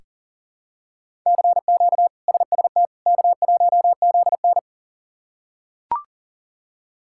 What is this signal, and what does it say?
Morse code, heard as a keyed tone, about 37 wpm, 700 Hz. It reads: CQ SST K1ZN
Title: K1USN SST QSO Practice - 40wpm